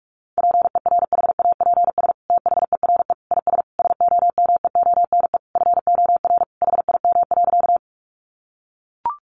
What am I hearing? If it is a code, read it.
Morse code, heard as a keyed tone, about 44 wpm, 700 Hz. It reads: PERHAPS THERE IS SOMEONE FOR HIM.
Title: Sentences from Top 600 Words 45wpm